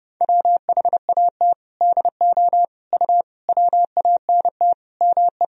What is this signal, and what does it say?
Morse code, heard as a keyed tone, about 30 wpm, 700 Hz. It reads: WHAT DO U WANT ME
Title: Sentences from Top 200 Words and Abbreviations 30wpm